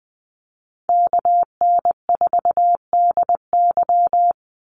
Morse code, heard as a keyed tone, about 20 wpm, 700 Hz. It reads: KN4DY
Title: USA Call Signs 20wpm